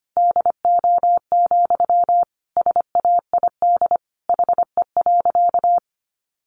Morse code, heard as a keyed tone, about 25 wpm, 700 Hz. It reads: DO, SAID HE.
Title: Sentences from Top 100 Words 25wpm